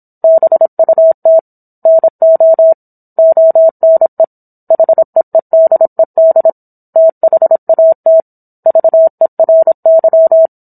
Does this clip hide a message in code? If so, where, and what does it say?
Yes: Morse code from 0.2 to 10.5 s, heard as a keyed tone, about 27 wpm, 650 Hz. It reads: BUT NO ONE HEEDED THAT VERY